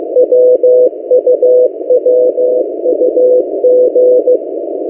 Morse code, heard as a keyed tone, about 15 wpm, 500 Hz. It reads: WUWUG
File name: M08a- -1355utc-8096khz